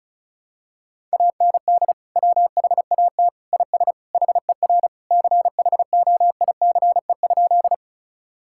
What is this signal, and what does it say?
Morse code, heard as a keyed tone, about 34 wpm, 700 Hz. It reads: AND WHAT IS HER CHOICE?
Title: Sentences from Top 1000 Words - Review Part 6 - 35wpm